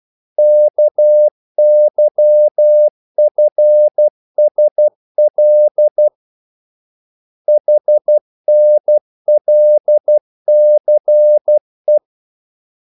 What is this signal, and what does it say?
Morse code, heard as a keyed tone, about 12 wpm, 600 Hz. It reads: KYFSL HNLCE
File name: code-exemple